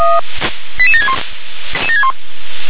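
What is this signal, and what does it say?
Five-tone selective call (ZVEI), heard as quick runs of five tones, tones of about 70 ms. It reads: 88761
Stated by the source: ZVEI five-tone selective call received on an SDR.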